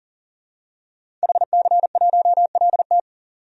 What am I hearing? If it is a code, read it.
Morse code, heard as a keyed tone, about 40 wpm, 700 Hz. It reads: HC1LT
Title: DXCC-100 Call Signs - 5 Characters - 40wpm